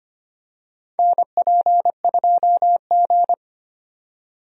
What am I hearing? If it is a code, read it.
Morse code, heard as a keyed tone, about 25 wpm, 700 Hz. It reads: NP2G
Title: USA Call Signs - 4 Characters - 25wpm